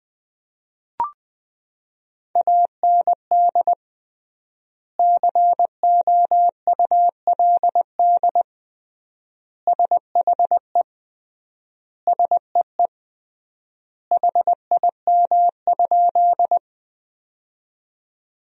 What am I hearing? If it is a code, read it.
Morse code, heard as a keyed tone, about 21 wpm, 700 Hz. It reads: AND COULD SHE SEE HIM?
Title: 3x Word-Spacing - Sentences from Top 100 Words - 20wpm